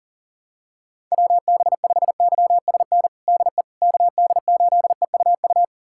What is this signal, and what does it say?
Morse code, heard as a keyed tone, about 40 wpm, 700 Hz. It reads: WB5YSN DE KD8EUU
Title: USA Callsign DE Callsign - 6 Characters - 40wpm